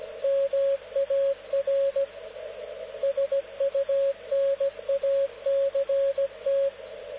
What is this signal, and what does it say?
Morse code, heard as a keyed tone, about 17 wpm, 550 Hz, with radio noise. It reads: MAR SUNACT
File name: luxorion / Radio / dk0wcy-spaceweather-10144khz